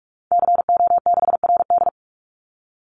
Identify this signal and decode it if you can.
Morse code, heard as a keyed tone, about 44 wpm, 700 Hz. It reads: CO6RD
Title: Call Signs Repeated 3x - 45wpm